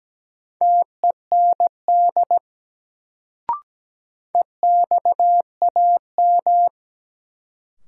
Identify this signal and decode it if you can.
Morse code, heard as a keyed tone, about 17 wpm, 700 Hz. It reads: TEND EXAM
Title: Sentences from Top 1000 Words - Review Part 5 - New Word Drill - 17wpm